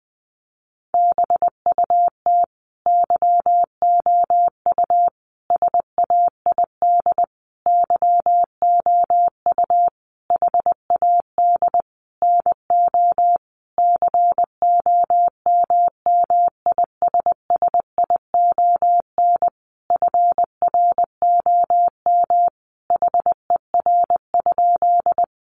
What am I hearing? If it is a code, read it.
Morse code, heard as a keyed tone, about 20 wpm, 700 Hz. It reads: BUT YOU SAID YOU HAD NO COMMISSION FROM HER?
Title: Sentences from Top 1400 Words 20wpm